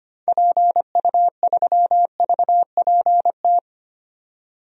Morse code, heard as a keyed tone, about 25 wpm, 700 Hz. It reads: PU3VPT
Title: DXCC-100 Call Signs - 6 Characters - 25wpm